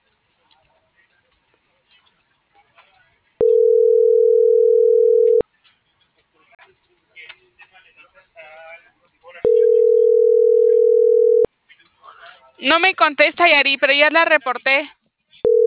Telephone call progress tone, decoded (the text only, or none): ringback tone